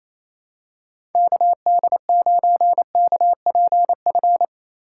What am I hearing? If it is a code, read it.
Morse code, heard as a keyed tone, about 28 wpm, 700 Hz. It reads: KD9KPF
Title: USA Call Signs - 6 Characters - 6x Character Spacing - 28wpm